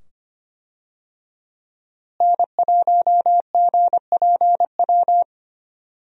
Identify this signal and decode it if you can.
Morse code, heard as a keyed tone, about 25 wpm, 700 Hz. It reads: N1GPW
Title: USA Call Signs - 5 Characters - Repeated 2x - 25wpm